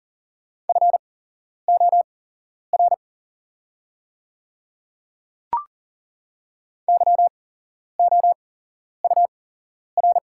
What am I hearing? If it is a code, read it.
Morse code, heard as a keyed tone, about 40 wpm, spaced as for 11 wpm, 690 Hz. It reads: FOR YOUR
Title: 8x Character Spacing - Top 100 Words - 40wpm